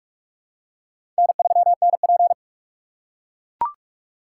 Morse code, heard as a keyed tone, about 42 wpm, 700 Hz. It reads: N3NP
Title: USA Call Signs - 4 Characters - 6x Character Spacing - 45wpm